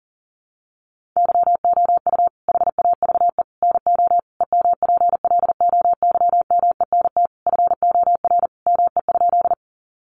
Morse code, heard as a keyed tone, about 40 wpm, 700 Hz. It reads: YOU HAVE NO EMPLOYMENT FOR ME?
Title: Sentences from Top 1500 Words 40wpm